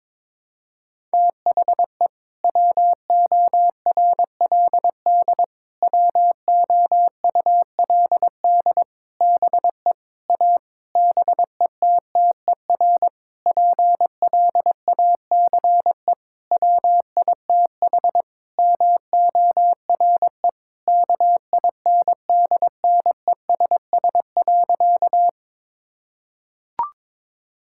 Morse code, heard as a keyed tone, about 22 wpm, 700 Hz. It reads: THE WORLD WOULD BE A BETTER PLACE WITH MORE KINDNESS.
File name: Sentences with 7-Word Phrases - 22wpm